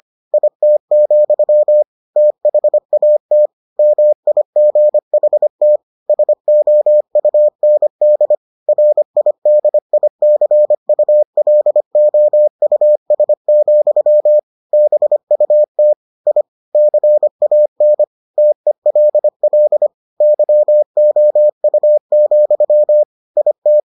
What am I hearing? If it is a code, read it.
Morse code, heard as a keyed tone, about 25 wpm, 600 Hz. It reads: IT, THAT MIGHT SOUND RIDICULOUS, BUT I CAN TELL YOU, IT